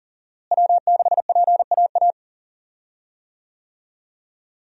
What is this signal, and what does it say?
Morse code, heard as a keyed tone, about 38 wpm, 700 Hz. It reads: W6PAA
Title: USA Call Signs - 5 Characters - Repeated 2x - 40wpm